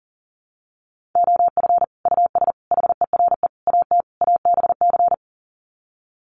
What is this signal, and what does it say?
Morse code, heard as a keyed tone, about 40 wpm, 700 Hz. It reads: OF US HERE AT ABC
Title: Sets of 5 Words 40wpm